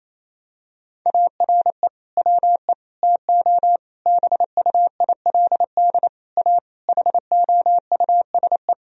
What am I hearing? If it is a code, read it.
Morse code, heard as a keyed tone, about 28 wpm, 700 Hz. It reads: ARE WE TO BUILD A HOUSE
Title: Sentences from Top 1000 Words - Review Part 1 - 28wpm